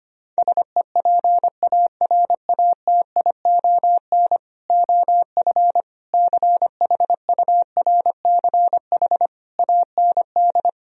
Morse code, heard as a keyed tone, about 24 wpm, 700 Hz. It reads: SEPARATION OF CHURCH AND
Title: Speed-Racing - Sets of 5 Words 25wpm